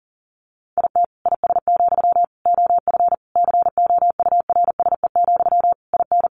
Morse code, heard as a keyed tone, about 40 wpm, 700 Hz. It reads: IT IS, OF COURSE, IN
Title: Sentences from Top 1200 Words 40wpm